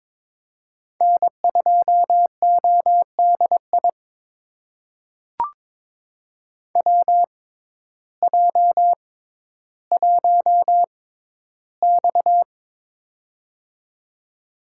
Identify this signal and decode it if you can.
Morse code, heard as a keyed tone, about 22 wpm, spaced as for 12 wpm, 700 Hz. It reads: N2ODI WJ1X
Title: USA Call Signs - 6x Character Spacing - 22wpm